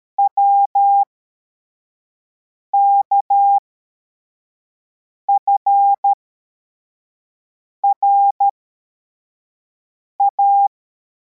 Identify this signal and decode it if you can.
Morse code, heard as a keyed tone, about 13 wpm, spaced as for 4 wpm, 800 Hz. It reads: WKFRA
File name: formation / cw / Text62